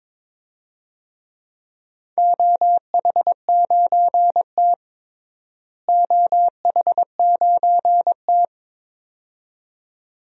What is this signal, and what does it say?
Morse code, heard as a keyed tone, about 22 wpm, 700 Hz. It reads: OH9T OH9T
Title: DXCC-100 Call Signs - 4 Characters - Repeated 2x - 22wpm